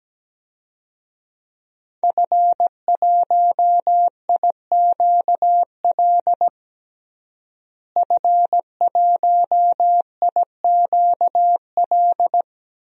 Morse code, heard as a keyed tone, about 17 wpm, 700 Hz. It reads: F1IQL F1IQL
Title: DXCC-100 Call Signs - 5 Characters - Repeated 2x - 17wpm